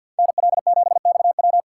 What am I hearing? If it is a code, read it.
Morse code, heard as a keyed tone, about 60 wpm, 700 Hz. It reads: NL7XW
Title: Speed Racing - USA 2x2 Call Signs 50wpm